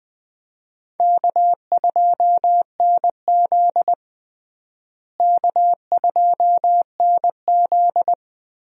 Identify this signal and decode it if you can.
Morse code, heard as a keyed tone, about 20 wpm, 700 Hz. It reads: K2NZ K2NZ
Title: USA Call Signs - 4 Characters - Repeated 3x - 20wpm